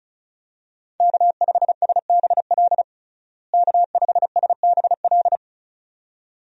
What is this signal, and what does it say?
Morse code, heard as a keyed tone, about 36 wpm, 700 Hz. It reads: K5SBL K5SBL
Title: USA Call Signs - 5 Characters - Repeated 2x - 35wpm